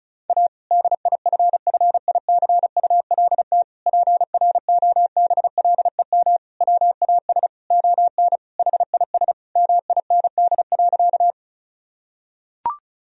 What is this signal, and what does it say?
Morse code, heard as a keyed tone, about 36 wpm, 700 Hz. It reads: A DIFFICULT PROBLEM WAS ON HIS MIND.
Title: Sentences from Top 1000 Words - Review Part 2 - 35wpm